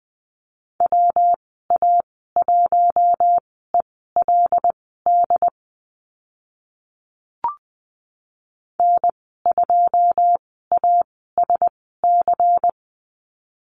Morse code, heard as a keyed tone, about 20 wpm, spaced as for 14 wpm, 700 Hz. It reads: WA1ELD N2ASC
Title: 140 Lesson 23g - Letter f - Call Signs 20wpm